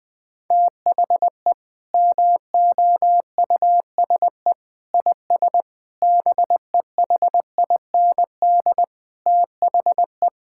Morse code, heard as a keyed tone, about 20 wpm, 700 Hz. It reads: THE MOUSE IS BEHIND THE
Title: Sentences from Top 1000 Words - Review Part 5 - 20wpm